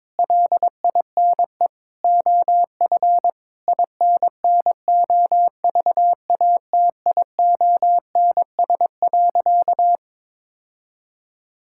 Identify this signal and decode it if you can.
Morse code, heard as a keyed tone, about 22 wpm, 700 Hz. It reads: LINE OF INNOVATIONS.
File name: practice / 22wpm / Sentences with 8-Word Phrases - 22wpm